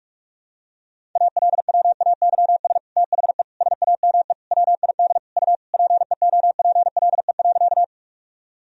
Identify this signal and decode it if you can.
Morse code, heard as a keyed tone, about 46 wpm, 700 Hz. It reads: ALWAYS THE SAME WID U PEOPLE.
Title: Sentences from Top 300 Words and Abbreviations 45wpm